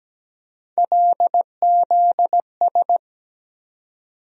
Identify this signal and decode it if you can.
Morse code, heard as a keyed tone, about 17 wpm, 700 Hz. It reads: LZS
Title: USA Call Sign Postfixes - 3 Letters 17wpm